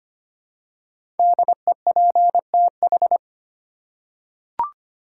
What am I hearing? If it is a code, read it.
Morse code, heard as a keyed tone, about 25 wpm, 700 Hz. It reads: DEPTH
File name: 5 Letter Words - First 3 Letters Given - 25wpm